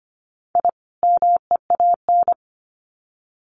Morse code, heard as a keyed tone, about 25 wpm, 700 Hz. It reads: I MEAN
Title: Sets of 2 Words 25wpm